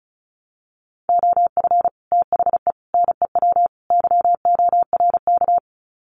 Morse code, heard as a keyed tone, about 34 wpm, 700 Hz. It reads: OF THE NEW YORK
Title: Sets of 4 Words 35wpm